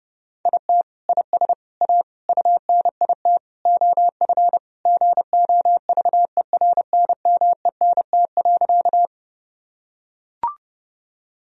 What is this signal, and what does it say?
Morse code, heard as a keyed tone, about 30 wpm, 700 Hz. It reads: IT IS A UNIT OF GOVERNMENT.
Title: Sentences from Top 1000 Words - Review Part 2 - 30wpm